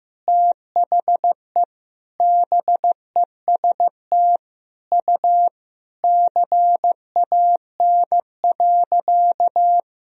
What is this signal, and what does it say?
Morse code, heard as a keyed tone, about 16 wpm, 700 Hz. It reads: THE BEST U CAN.
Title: Sentences from Top 500 Words and Abbreviations 15wpm